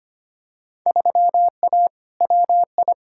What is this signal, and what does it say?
Morse code, heard as a keyed tone, about 25 wpm, 700 Hz. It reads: 3A WI
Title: Field Day Exchanges - 25wpm